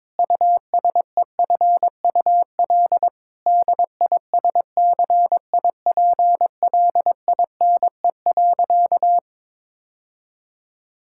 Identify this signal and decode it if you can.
Morse code, heard as a keyed tone, about 22 wpm, 700 Hz. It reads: USEFUL DISCIPLINE.
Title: Sentences from Top 1000 Words - Review Part 6 - 22wpm